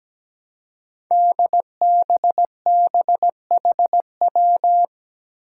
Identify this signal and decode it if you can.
Morse code, heard as a keyed tone, about 17 wpm, 700 Hz. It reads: DBBHW
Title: Five Challenging Letter-Number Combinations 17wpm